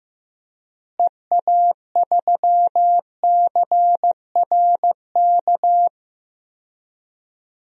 Morse code, heard as a keyed tone, about 15 wpm, 700 Hz. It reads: EA3CRK
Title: DXCC-100 Call Signs - 6 Characters - Repeated 2x - 15wpm